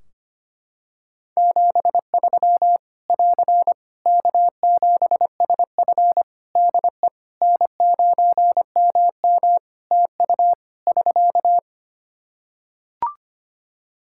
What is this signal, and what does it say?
Morse code, heard as a keyed tone, about 25 wpm, 700 Hz. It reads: 73 <AR> K7SF DE N9MM TU <SK>
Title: K1USN SST QSO Practice 25wpm